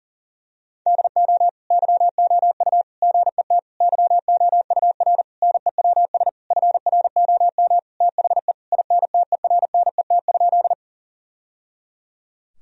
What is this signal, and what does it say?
Morse code, heard as a keyed tone, about 37 wpm, 700 Hz. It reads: DO YOU GET YOUR NEWS FROM THE INTERNET?
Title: Sentences from Top 1000 Words - Review Part 4 - 40wpm